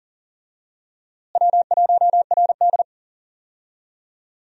Morse code, heard as a keyed tone, about 37 wpm, 700 Hz. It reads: W1RD
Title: USA Call Signs - 4 Characters - Repeated 2x - 40wpm